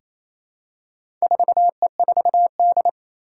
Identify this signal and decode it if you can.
Morse code, heard as a keyed tone, about 28 wpm, 700 Hz. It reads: 4E4D
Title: Four Challenging Letter-Number Combinations 28wpm